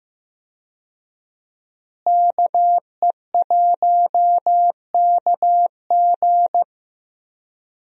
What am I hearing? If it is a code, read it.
Morse code, heard as a keyed tone, about 15 wpm, 700 Hz. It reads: KE1KG
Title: USA Call Signs - 5 Characters - Repeated 2x - 15wpm